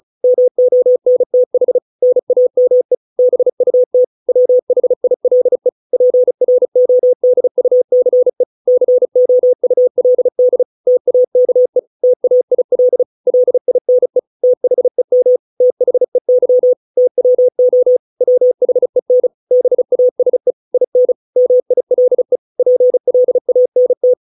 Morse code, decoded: MONTH NAME BUT WHILE PRODUCE COULD TAKE TAIL LINE THEM THEY TWO WHEN BASE IN MILE PLANT

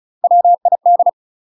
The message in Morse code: WID

